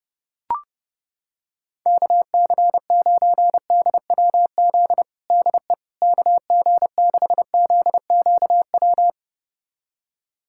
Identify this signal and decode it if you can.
Morse code, heard as a keyed tone, about 32 wpm, 700 Hz. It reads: KC9DWZ DE KG6ZQW
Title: USA Callsign DE Callsign - 6 Characters - 30wpm